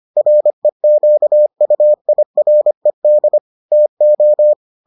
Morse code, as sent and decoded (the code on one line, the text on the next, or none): .-. --.- ..- .. .-. -.. / - ---
REQUIRED TO